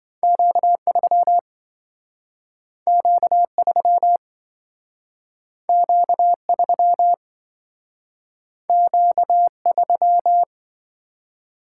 Morse code, heard as a keyed tone, about 28 wpm, 700 Hz. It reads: Q3 Q3 Q3 Q3